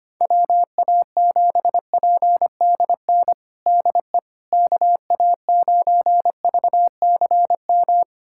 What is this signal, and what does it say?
Morse code, heard as a keyed tone, about 25 wpm, 700 Hz. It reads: WA7PDN DE KA9VCM
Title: USA Callsign DE Callsign 25wpm